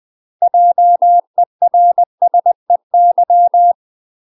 Morse code, heard as a keyed tone, about 20 wpm, 700 Hz. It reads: JERSEY
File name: UK MASTER QSOs TRAINING/20WPM